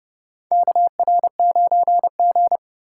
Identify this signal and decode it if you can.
Morse code, heard as a keyed tone, about 30 wpm, 700 Hz. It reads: KR9G
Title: USA Call Signs - 8x Character Spacing - 30wpm